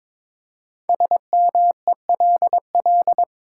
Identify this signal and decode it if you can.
Morse code, heard as a keyed tone, about 22 wpm, 700 Hz. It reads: SMELL
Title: Five Letter Words - 22wpm